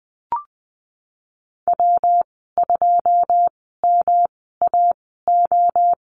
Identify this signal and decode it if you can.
Morse code, heard as a keyed tone, about 20 wpm, spaced as for 15 wpm, 700 Hz. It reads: W2MAO
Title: 173 Lesson 28g - Number 9 - Call Signs 20wpm